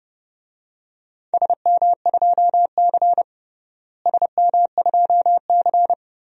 Morse code, heard as a keyed tone, about 30 wpm, 700 Hz. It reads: SM2C SM2C
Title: DXCC-100 Call Signs - 4 Characters - Repeated 3x - 30wpm